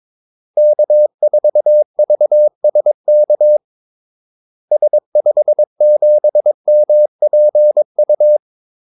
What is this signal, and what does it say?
Morse code, heard as a keyed tone, about 22 wpm, 600 Hz. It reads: K4VSK S57MPU
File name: Callsigns-22wpm